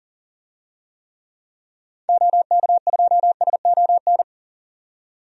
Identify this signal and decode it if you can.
Morse code, heard as a keyed tone, about 40 wpm, 700 Hz. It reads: OK2SON
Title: DXCC-100 Call Signs - 6 Characters - 40wpm